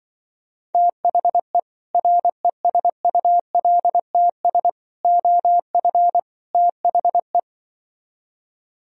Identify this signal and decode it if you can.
Morse code, heard as a keyed tone, about 25 wpm, 700 Hz. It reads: THE RESULTS OF THE
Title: Speed-Racing - Sets of 4 Words 20wpm